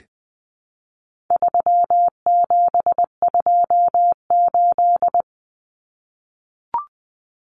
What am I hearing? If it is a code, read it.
Morse code, heard as a keyed tone, about 20 wpm, 700 Hz. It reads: 3728
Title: Common Cut Numbers - 4-Digit 20wpm